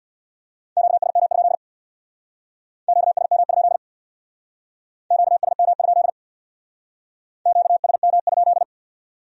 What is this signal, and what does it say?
Morse code, heard as a keyed tone, about 61 wpm, 700 Hz. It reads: QSM? QSM? QSM? QSM?
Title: Speed-Racing Q-Codes - 50wpm